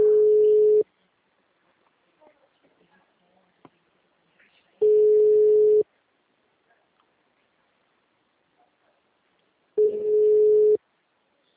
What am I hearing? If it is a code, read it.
Ringback tone (Europe, 425 Hz), with speech.